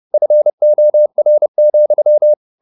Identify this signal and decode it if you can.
Morse code, heard as a keyed tone, about 30 wpm, 600 Hz. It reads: FOR,
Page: Ferie 30 wpm